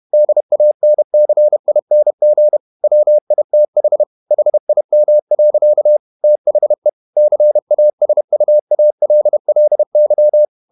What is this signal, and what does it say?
Morse code, heard as a keyed tone, about 31 wpm, 600 Hz. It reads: DANCING WITH HIM. THE CASUALLY
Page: Never 31wpm | CW med Gnister